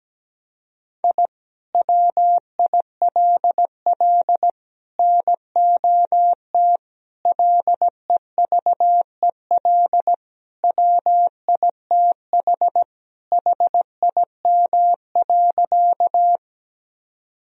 Morse code, heard as a keyed tone, about 17 wpm, 700 Hz. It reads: I WILL NOT LEVEL WITH HIM.
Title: Sentences from Top 1000 Words - Review Part 4 - 17wpm